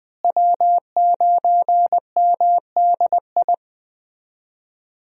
Morse code, heard as a keyed tone, about 19 wpm, 700 Hz. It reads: W9MDI
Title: USA Call Signs - 5 Characters - Repeated 3x - 20wpm